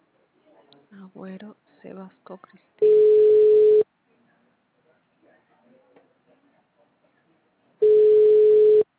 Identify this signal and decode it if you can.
Ringback tone (Europe, 425 Hz), with speech.